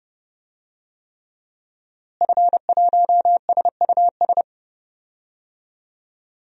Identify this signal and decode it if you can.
Morse code, heard as a keyed tone, about 30 wpm, 700 Hz. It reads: F1SUS
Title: DXCC-100 Call Signs - 5 Characters - Repeated 3x - 30wpm